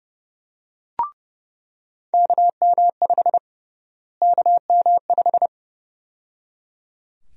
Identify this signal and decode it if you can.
Morse code, heard as a keyed tone, about 30 wpm, 700 Hz. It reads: KM5 KM5
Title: USA Call Sign Prefixes - Repeated 2x - 30wpm